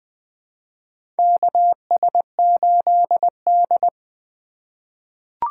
Morse code, heard as a keyed tone, about 20 wpm, 700 Hz. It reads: KS8D
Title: USA Call Signs - 3x Character Spacing - 20wpm